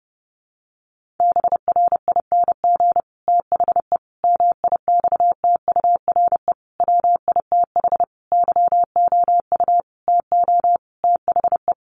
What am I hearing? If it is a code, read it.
Morse code, heard as a keyed tone, about 30 wpm, 700 Hz. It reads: BRING THE MIXTURE WITH YOU TO THE
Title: Sentences from Top 1500 Words 30wpm